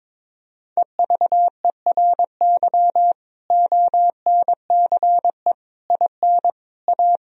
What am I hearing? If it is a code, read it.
Morse code, heard as a keyed tone, about 23 wpm, 700 Hz. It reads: EVERY ONCE IN A
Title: Speed-Racing - Sets of 5 Words 22wpm